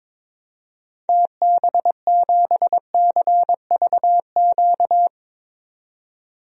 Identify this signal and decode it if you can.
Morse code, heard as a keyed tone, about 22 wpm, 700 Hz. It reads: TB7CVQ
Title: DXCC-100 Call Signs - 6 Characters - 22wpm